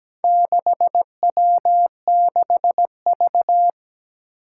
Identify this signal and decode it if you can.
Morse code, heard as a keyed tone, about 17 wpm, 700 Hz. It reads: 6W6V
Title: Four Challenging Letter-Number Combinations 17wpm